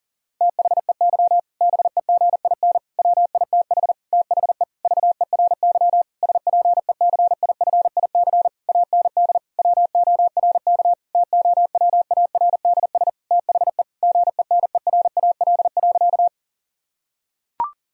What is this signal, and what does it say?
Morse code, heard as a keyed tone, about 40 wpm, 700 Hz. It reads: THEY BEGIN WITH THE VERY SPECIFIC AND WORK TOWARDS THE GENERAL.
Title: Sentences from Top 1000 Words - Review Part 3 - 40wpm